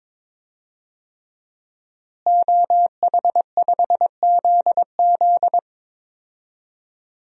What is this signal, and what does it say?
Morse code, heard as a keyed tone, about 22 wpm, 700 Hz. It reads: OH5ZZ
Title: DXCC-100 Call Signs - 5 Characters - Repeated 3x - 22wpm